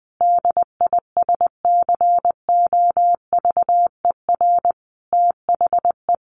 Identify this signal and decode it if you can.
Morse code, heard as a keyed tone, about 19 wpm, 700 Hz. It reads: DISCOVER THE
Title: Sentences from Top 1200 Words 20wpm